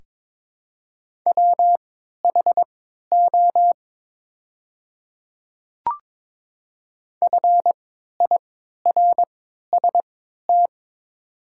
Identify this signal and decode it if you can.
Morse code, heard as a keyed tone, about 21 wpm, spaced as for 12 wpm, 700 Hz. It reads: WHO FIRST
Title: 3x Character Spacing - Top 100 Words - 22wpm